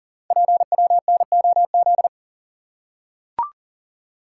Morse code, heard as a keyed tone, about 40 wpm, 700 Hz. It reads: PWNOZ
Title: Code Groups - 5 Letters - 40wpm